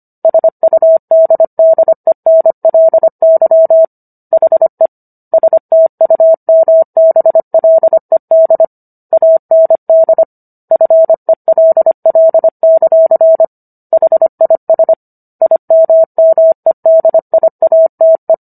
Morse code, heard as a keyed tone, about 25 wpm, 650 Hz. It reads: SUDDENLY HE STUMBLED AND FELL; HIS IMMEDIATE